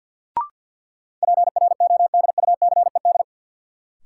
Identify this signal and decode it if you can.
Morse code, heard as a keyed tone, about 52 wpm, 700 Hz. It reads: PRODUCED
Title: Sentences from Top 1000 Words - Review Part 6 - New Word Drill - 50wpm